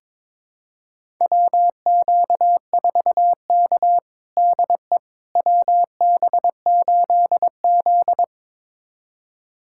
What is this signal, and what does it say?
Morse code, heard as a keyed tone, about 21 wpm, 700 Hz. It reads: WQ4K DE WB8Z
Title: USA Callsign DE Callsign - 4 Characters - 22wpm